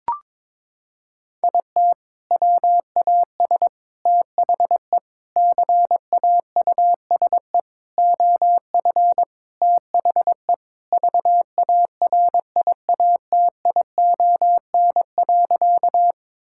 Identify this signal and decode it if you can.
Morse code, heard as a keyed tone, about 22 wpm, 700 Hz. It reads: IT WAS THE CAUSE OF THE VARIATION.
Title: Sentences from Top 1600 Words 22wpm